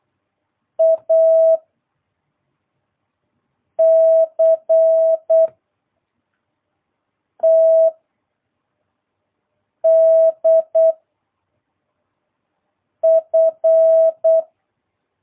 Morse code, heard as a keyed tone, about 8 wpm, 660 Hz. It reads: A C T D F